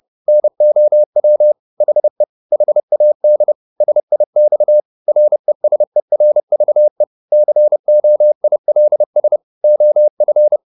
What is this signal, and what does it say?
Morse code, heard as a keyed tone, about 30 wpm, 600 Hz. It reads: NOW HE HAD SIX RESERVE COILS OF
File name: the old man and the sea / the old man and the sea 1 30wpm